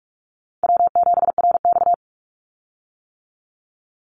Morse code, heard as a keyed tone, about 46 wpm, 700 Hz. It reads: W7RX
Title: Call Signs - 45wpm